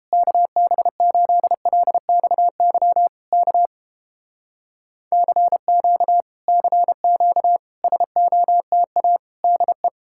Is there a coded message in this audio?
Yes: Morse code from 0.1 to 9.9 s, heard as a keyed tone, about 30 wpm, 700 Hz. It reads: KB8LXY K CQ CQ SOTA DE